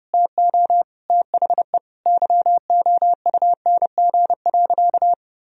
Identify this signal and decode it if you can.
Morse code, heard as a keyed tone, about 30 wpm, 700 Hz. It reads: TO THE YOUNG.
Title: Sentences from Top 1000 Words - Review Part 1 - 30wpm